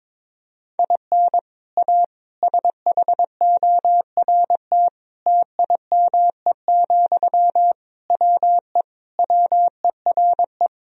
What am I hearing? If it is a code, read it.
Morse code, heard as a keyed tone, about 22 wpm, 700 Hz. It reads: IN A SHORT TIME, WE WERE
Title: Sentences from Top 1000 Words - Review Part 2 - 22wpm